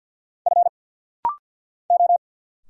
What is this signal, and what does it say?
Morse code, heard as a keyed tone, about 50 wpm, 700 Hz. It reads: F O
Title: Single Letters - Rapid-Fire 50wpm